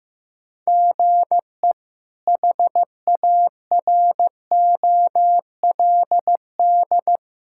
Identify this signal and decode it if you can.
Morse code, heard as a keyed tone, about 15 wpm, 700 Hz. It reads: GE HAROLD
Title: K1USN SST QSO Practice - 15wpm